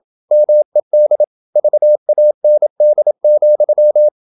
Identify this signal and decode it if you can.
Morse code, heard as a keyed tone, about 27 wpm, 600 Hz. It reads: MED VAND,
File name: Ferie QRQ 27wpm